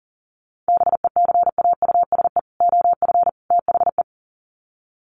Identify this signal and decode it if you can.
Morse code, heard as a keyed tone, about 40 wpm, 700 Hz. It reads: BECAUSE OF THE
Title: Sets of 3 Words 40wpm